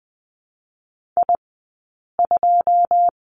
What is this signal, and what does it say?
Morse code, heard as a keyed tone, about 19 wpm, spaced as for 8 wpm, 700 Hz. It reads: I2
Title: 080 Lesson 14d - Number 2 - Two Characters 20wpm